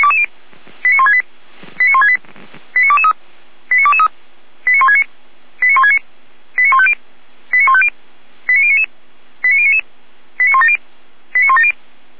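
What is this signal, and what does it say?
Five-tone selective call (ZVEI), heard as quick runs of five tones, tones of about 70 ms. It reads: 89168 89168 89223 89223 89169 89169 89150 89150 89099 89099 89170 89170
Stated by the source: ZVEI five-tone call 89167 received by SDR, recorded by Feuerwehr Orlamünde.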